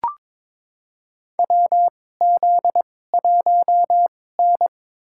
Morse code, heard as a keyed tone, about 22 wpm, spaced as for 16 wpm, 700 Hz. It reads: WZ1N
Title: USA Call Signs - 2x Character Spacing - 22wpm